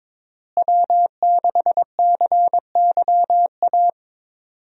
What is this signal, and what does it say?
Morse code, heard as a keyed tone, about 22 wpm, 700 Hz. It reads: W6CYA
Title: USA Call Signs - 5 Characters - Repeated 2x - 22wpm